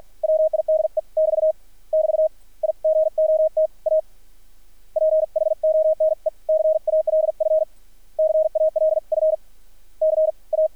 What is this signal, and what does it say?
Morse code, heard as a keyed tone, about 53 wpm, spaced as for 44 wpm, 630 Hz. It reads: WIDE <BT> <BT> I GOT A PHONE CALF CALF CA
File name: archived / CRSnet